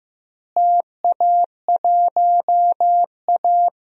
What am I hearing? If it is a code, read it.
Morse code, heard as a keyed tone, about 16 wpm, 700 Hz. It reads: TA1A
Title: DXCC-100 Call Signs - 4 Characters - Repeated 3x - 15wpm